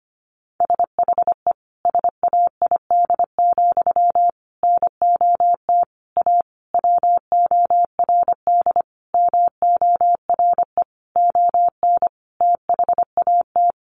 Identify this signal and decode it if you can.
Morse code, heard as a keyed tone, about 25 wpm, 700 Hz. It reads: SHE SAID, NOT A WORD MORE ON THAT
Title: Sentences from Top 800 Words 25wpm